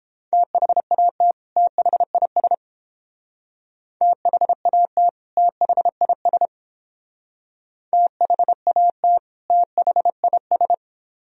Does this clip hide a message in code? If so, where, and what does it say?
Yes: Morse code from 0.3 to 10.7 s, heard as a keyed tone, about 29 wpm, 700 Hz. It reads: THAT THIS THAT THIS THAT THIS